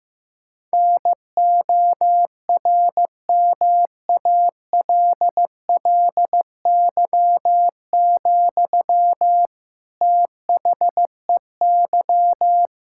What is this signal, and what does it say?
Morse code, heard as a keyed tone, about 16 wpm, 700 Hz. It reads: NORMALLY, THEY
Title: Sentences from Top 1000 Words - Review Part 6 - 15wpm